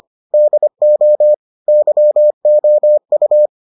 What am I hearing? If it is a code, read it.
Morse code, heard as a keyed tone, about 25 wpm, 600 Hz. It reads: DO YOU